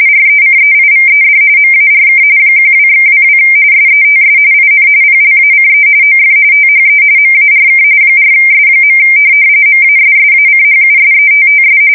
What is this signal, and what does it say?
RTTY, about 45.45 baud. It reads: RECEIVED MARK 12/22 SETS IN PLACE OF THE MARK 4S. MASSACHUSETTS
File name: fsk170 / USS-Massachusetts